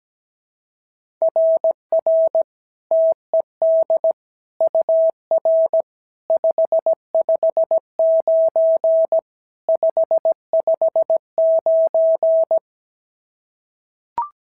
Morse code, heard as a keyed tone, about 16 wpm, 650 Hz. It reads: RR TED UR 559 559
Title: SOTA QSO Practice 17wpm